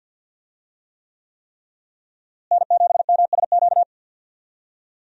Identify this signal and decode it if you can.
Morse code, heard as a keyed tone, about 45 wpm, 700 Hz. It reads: N7MSQ
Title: USA Call Signs - 5 Characters - Repeated 3x - 50wpm